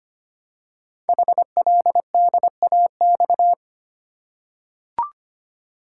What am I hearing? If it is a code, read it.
Morse code, heard as a keyed tone, about 25 wpm, 700 Hz. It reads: HLDAX
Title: Code Groups - 5 Letters - 25wpm